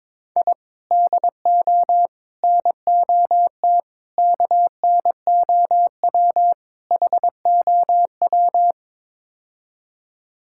Morse code, heard as a keyed tone, about 21 wpm, 700 Hz. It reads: I DO NOT KNOW HOW
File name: practice / 22wpm / Build A Phrase - 5-Word Phrases - 22wpm